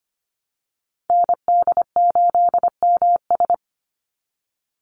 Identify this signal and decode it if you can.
Morse code, heard as a keyed tone, about 25 wpm, 700 Hz. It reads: ND8MS
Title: Call Signs Repeated 2x - 25wpm